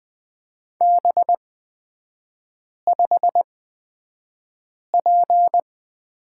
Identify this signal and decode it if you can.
Morse code, heard as a keyed tone, about 20 wpm, spaced as for 5 wpm, 700 Hz. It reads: B5P
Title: 224 Lesson 36e - BT - Three Characters 20wpm